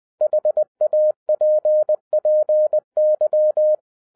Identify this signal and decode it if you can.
Morse code, heard as a keyed tone, about 20 wpm, 610 Hz. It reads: HAPPY